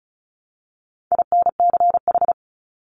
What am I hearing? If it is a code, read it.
Morse code, heard as a keyed tone, about 34 wpm, 700 Hz. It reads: INCH
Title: Top 500 Words Repeated 35wpm